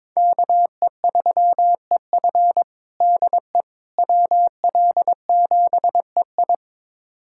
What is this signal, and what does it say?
Morse code, heard as a keyed tone, about 22 wpm, 700 Hz. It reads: KE3EF DE WL7EI
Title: USA Callsign DE Callsign - 5 Characters - 22wpm